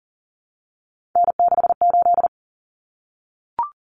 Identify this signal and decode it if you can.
Morse code, heard as a keyed tone, about 39 wpm, 700 Hz. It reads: N68
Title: Common Cut Numbers - 3-Digit 40wpm